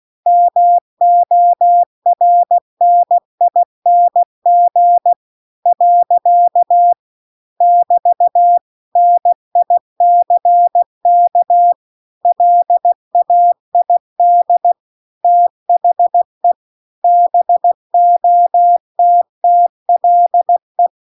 Morse code, heard as a keyed tone, about 16 wpm, 700 Hz. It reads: MORNING . <BT> NICK LAID THE BOTTLE